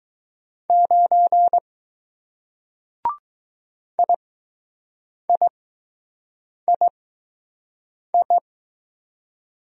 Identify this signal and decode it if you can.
Morse code, heard as a keyed tone, about 20 wpm, spaced as for 6 wpm, 700 Hz. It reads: 9 IIII